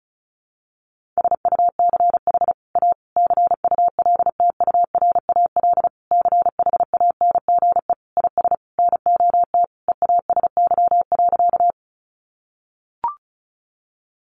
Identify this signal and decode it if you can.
Morse code, heard as a keyed tone, about 34 wpm, 700 Hz. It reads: SUCH A CULTURAL CHANGE IS NOT EASY.
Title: Sentences from Top 1500 Words 35wpm